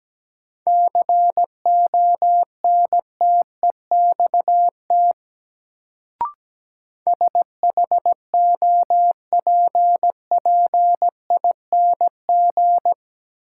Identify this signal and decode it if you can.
Morse code, heard as a keyed tone, about 17 wpm, 700 Hz. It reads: CONTEXT SHOPPING
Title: Sentences from Top 1000 Words - Review Part 7 - New Word Drill - 17wpm